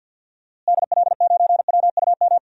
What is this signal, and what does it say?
Morse code, heard as a keyed tone, about 50 wpm, 700 Hz. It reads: DL9WUM
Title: DXCC-100 Call Signs - 6 Characters - Repeated 2x - 50wpm